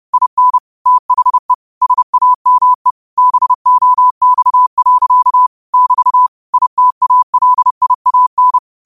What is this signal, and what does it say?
Morse code, heard as a keyed tone, about 30 wpm, 1000 Hz. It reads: IN THE SAME BOX. <BT> ITALIAN
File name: qotdfiles / 30wpm